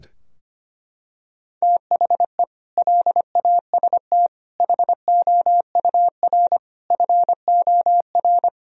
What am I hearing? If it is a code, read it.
Morse code, heard as a keyed tone, about 25 wpm, 700 Hz. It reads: THE LAST HOUR FOR
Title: Sentences from Top 1000 Words - Review Part 4 - 25wpm